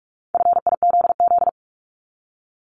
Morse code, heard as a keyed tone, about 46 wpm, 700 Hz. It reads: FIZZ
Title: Multi-letter Phonemes Repeated 3x 45wpm